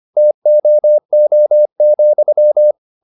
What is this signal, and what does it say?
Morse code, heard as a keyed tone, about 25 wpm, 600 Hz. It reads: TOO,
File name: the old man and the sea part 1 25wpm